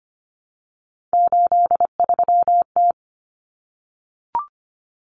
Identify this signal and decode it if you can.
Morse code, heard as a keyed tone, about 25 wpm, 700 Hz. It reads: 83T
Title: Common Cut Numbers - 3-Digit 25wpm